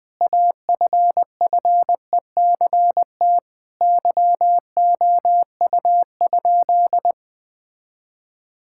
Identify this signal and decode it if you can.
Morse code, heard as a keyed tone, about 20 wpm, 700 Hz. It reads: AFFECT YOU?
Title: Sentences from Top 1000 Words - Review Part 6 - 20wpm